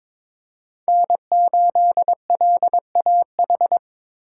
Morse code, heard as a keyed tone, about 22 wpm, 700 Hz. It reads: N8LAH
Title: USA Call Signs - 5 Characters - 2x Character Spacing - 22wpm